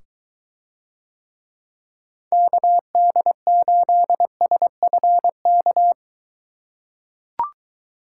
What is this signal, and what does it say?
Morse code, heard as a keyed tone, about 22 wpm, 700 Hz. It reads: KD8SFK